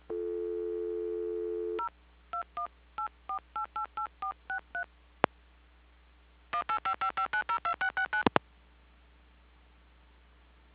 DTMF, heard as telephone touch tones, tones of about 100 ms. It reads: 0 2184888463 18555902669
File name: recordings / PortlandUnion Millennium